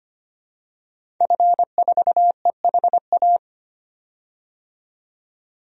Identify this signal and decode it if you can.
Morse code, heard as a keyed tone, about 25 wpm, 700 Hz. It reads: F4EHA
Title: DXCC-100 Call Signs - 5 Characters - 25wpm